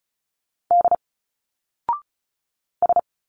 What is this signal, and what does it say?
Morse code, heard as a keyed tone, about 36 wpm, 700 Hz. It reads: D S
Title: Single Letters 35wpm